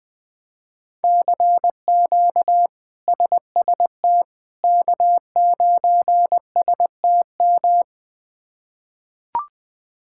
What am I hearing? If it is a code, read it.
Morse code, heard as a keyed tone, about 20 wpm, 700 Hz. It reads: CQ SST K9STM
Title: K1USN SST QSO Practice - 20wpm